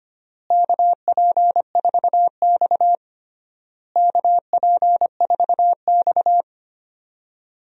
Morse code, heard as a keyed tone, about 25 wpm, 700 Hz. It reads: KP4X KP4X
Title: DXCC-100 Call Signs - 4 Characters - Repeated 3x - 25wpm